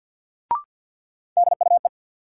Morse code, heard as a keyed tone, about 46 wpm, 700 Hz. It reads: DUE
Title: Sentences from Top 900 Words - New Word Drill - 50wpm